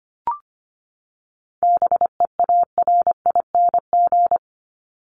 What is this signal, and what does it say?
Morse code, heard as a keyed tone, about 25 wpm, 700 Hz. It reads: BEARING
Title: ING Suffix - Top 100 Words with Base Words 25wpm